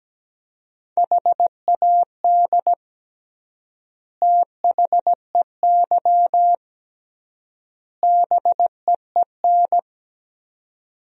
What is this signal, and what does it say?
Morse code, heard as a keyed tone, about 17 wpm, 700 Hz. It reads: HAD THEY BEEN
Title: 3x Word-Spacing - Sentences from Top 400 Words - 17wpm